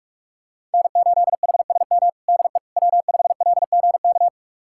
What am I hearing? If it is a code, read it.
Morse code, heard as a keyed tone, about 46 wpm, 700 Hz. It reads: N8HSM DE W5LGK
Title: USA Callsign DE Callsign - 5 Characters - 45wpm